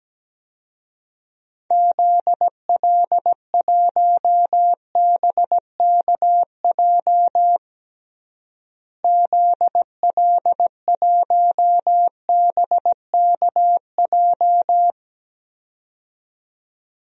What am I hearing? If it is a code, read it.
Morse code, heard as a keyed tone, about 17 wpm, 700 Hz. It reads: ZL1BKJ ZL1BKJ
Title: DXCC-100 Call Signs - 6 Characters - Repeated 2x - 17wpm